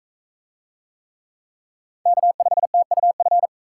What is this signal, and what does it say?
Morse code, heard as a keyed tone, about 38 wpm, 700 Hz. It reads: K5TUF